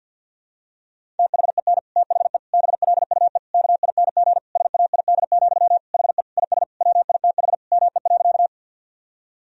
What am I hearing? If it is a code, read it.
Morse code, heard as a keyed tone, about 52 wpm, 700 Hz. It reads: THEN THE BLUE KING SAID, HE IS WITH ME.
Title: Sentences from Top 1000 Words - Review Part 3 - 50wpm